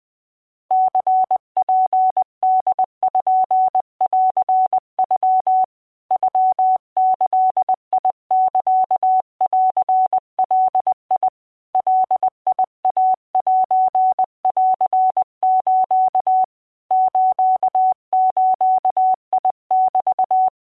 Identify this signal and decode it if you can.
Morse code (Japanese Wabun code), heard as a keyed tone, about 20 wpm, 750 Hz. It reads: ニツポンノ ノギサンガ ガイセンス スズメ
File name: mejirorosiya20wpm